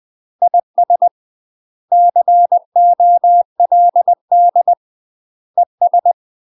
Morse code, heard as a keyed tone, about 20 wpm, 700 Hz. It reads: IS COLD ES
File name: UK MASTER QSOs TRAINING/20WPM